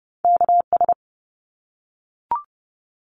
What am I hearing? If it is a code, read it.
Morse code, heard as a keyed tone, about 30 wpm, 700 Hz. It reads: KS
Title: ARRL Sections 30wpm